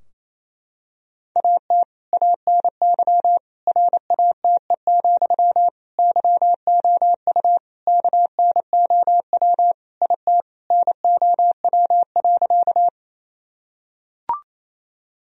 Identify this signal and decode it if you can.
Morse code, heard as a keyed tone, about 28 wpm, 700 Hz. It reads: AT ANY RATE, YOU KNOW IT NOW.
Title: Sentences from Top 1000 Words - Review Part 4 - 28wpm